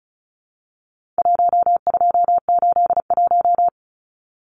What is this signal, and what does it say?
Morse code, heard as a keyed tone, about 34 wpm, 700 Hz. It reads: 1281
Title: Numbers - 4-Digit 35wpm